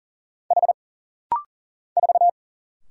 Morse code, heard as a keyed tone, about 40 wpm, 700 Hz. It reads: H 4
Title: Single Letter-Number - Rapid-Fire 40wpm